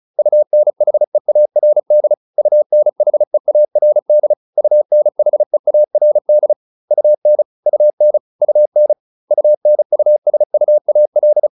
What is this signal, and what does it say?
Morse code, heard as a keyed tone, about 35 wpm, 600 Hz. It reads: UNHEARD UNHEARD UNHEARD UN UN UN UNUSUAL